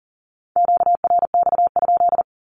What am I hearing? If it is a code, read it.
Morse code, heard as a keyed tone, about 40 wpm, 700 Hz. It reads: QRX?
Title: Q-Codes 40wpm